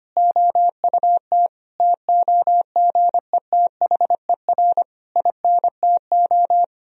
Morse code, heard as a keyed tone, about 25 wpm, 700 Hz. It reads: OUT TOGETHER INTO
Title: Sentences from Top 1000 Words - Review Part 3 - 25wpm